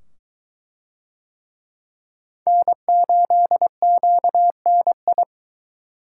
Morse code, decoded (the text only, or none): N8QNI